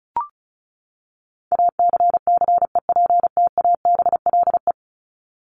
Morse code, heard as a keyed tone, about 34 wpm, 700 Hz. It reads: ACCEPTABLE
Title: Top 1501-1750 Words 35wpm